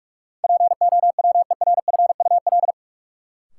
Morse code, heard as a keyed tone, about 46 wpm, 700 Hz. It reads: POWERFUL
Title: Sentences from Top 1000 Words - Review Part 6 - New Word Drill - 45wpm